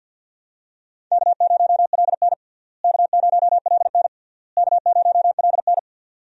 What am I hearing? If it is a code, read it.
Morse code, heard as a keyed tone, about 47 wpm, 700 Hz. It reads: K0LN K0LN K0LN
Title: USA Call Signs - 4 Characters - Repeated 3x - 50wpm